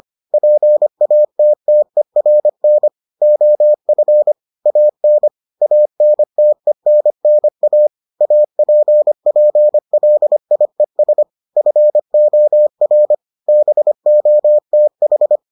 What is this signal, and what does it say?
Morse code, heard as a keyed tone, about 25 wpm, 600 Hz. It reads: PATTERN OF AN ANTENNA APPLIES FOR BOTH